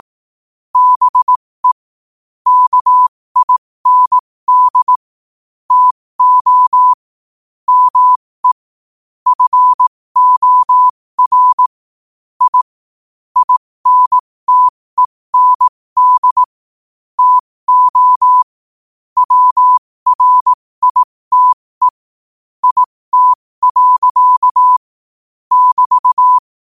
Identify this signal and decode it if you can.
Morse code, heard as a keyed tone, about 18 wpm, spaced as for 15 wpm, 1000 Hz. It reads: BE KIND TO ME FOR I INTEND TO WRITE IT. <BT>
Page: 15 wpm morse code quotes